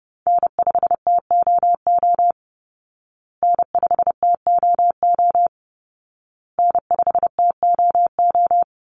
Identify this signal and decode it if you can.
Morse code, heard as a keyed tone, about 30 wpm, 700 Hz. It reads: N5TOO N5TOO N5TOO
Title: Call Signs Repeated 2x - 30wpm